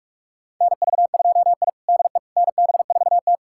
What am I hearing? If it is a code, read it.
Morse code, heard as a keyed tone, about 45 wpm, 700 Hz. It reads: NV2I DE NB4T
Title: USA Callsign DE Callsign - 4 Characters - 45wpm